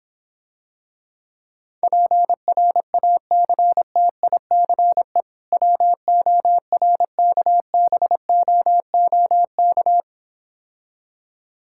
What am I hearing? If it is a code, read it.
Morse code, heard as a keyed tone, about 26 wpm, 700 Hz. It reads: PRACTICE WORKBOOK